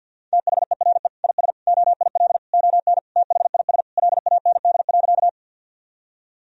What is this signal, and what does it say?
Morse code, heard as a keyed tone, about 50 wpm, 700 Hz. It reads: THERE IS OIL ON THIS LAND.
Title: Sentences from Top 1000 Words - Review Part 4 - 50wpm